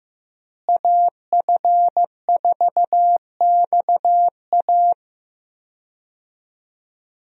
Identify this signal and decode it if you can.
Morse code, heard as a keyed tone, about 15 wpm, 700 Hz. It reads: AF4XA
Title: USA Call Signs - 5 Characters - Repeated 2x - 15wpm